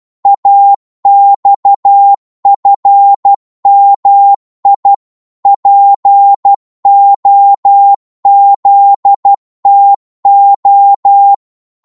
Morse code, heard as a keyed tone, about 12 wpm, 800 Hz. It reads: AXFMI POZTO
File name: Morse / ltr-12wpm